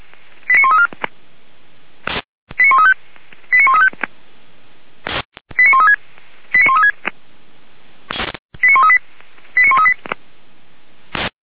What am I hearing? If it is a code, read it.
Five-tone selective call (ZVEI), heard as quick runs of five tones, tones of about 80 ms. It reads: 89134 89135 89135 89136 89136 89138 89138